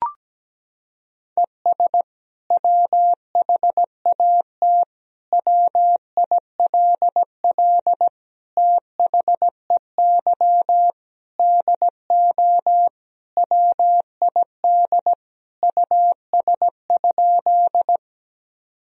Morse code, heard as a keyed tone, about 17 wpm, 700 Hz. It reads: ES WHAT WILL THEY DO WID US?
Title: Sentences from Top 200 Words and Abbreviations 17wpm